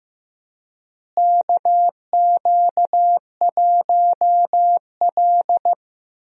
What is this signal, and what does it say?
Morse code, heard as a keyed tone, about 15 wpm, 700 Hz. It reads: KQ1L
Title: USA Call Signs - 4 Characters - Repeated 3x - 15wpm